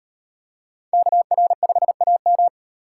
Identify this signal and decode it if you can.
Morse code, heard as a keyed tone, about 35 wpm, 700 Hz. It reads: KR5AM